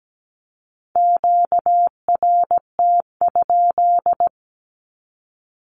Morse code, heard as a keyed tone, about 17 wpm, 700 Hz. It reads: QRT?
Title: Q-Codes with Spoken Meaning - 17wpm